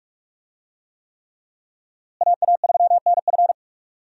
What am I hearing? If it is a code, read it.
Morse code, heard as a keyed tone, about 41 wpm, 700 Hz. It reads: AA3NF
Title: USA Call Signs - 5 Characters - Repeated 3x - 45wpm